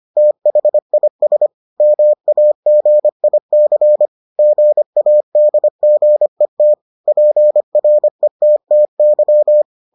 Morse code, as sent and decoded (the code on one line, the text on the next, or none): - .... .. ... / -- .- --. .. -.-. / --. .- -.. --. - / .--. .-. - - -.--
THIS MAGIC GADGET PRETTY